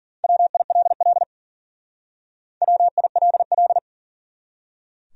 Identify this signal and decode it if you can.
Morse code, heard as a keyed tone, about 50 wpm, 700 Hz. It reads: WILL WILL
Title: Speed-Racing Top 100 Words 40wpm